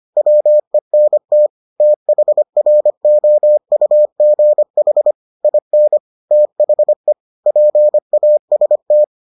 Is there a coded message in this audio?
Yes: Morse code from 0.2 to 9.0 s, heard as a keyed tone, about 25 wpm, 600 Hz. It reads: WENT THROUGH IN THE PAST